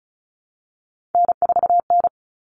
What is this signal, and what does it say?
Morse code, heard as a keyed tone, about 34 wpm, 700 Hz. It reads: N4N
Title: All Cut Numbers - 3-Digit 35wpm